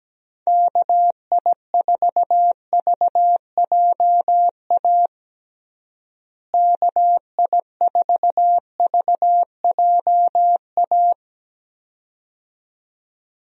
Morse code, heard as a keyed tone, about 17 wpm, 700 Hz. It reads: KI4VJA KI4VJA
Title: USA Call Signs - 6 Characters - Repeated 2x - 17wpm